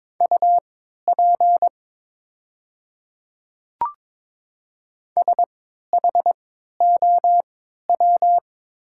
Morse code, heard as a keyed tone, about 22 wpm, spaced as for 13 wpm, 690 Hz. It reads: UP SHOW
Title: 3x Character Spacing - Top 200 Words - 22wpm